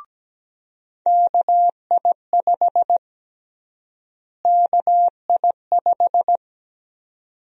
Morse code, heard as a keyed tone, about 17 wpm, 700 Hz. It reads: KI5 KI5
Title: USA Call Sign Prefixes - Repeated 2x - 17wpm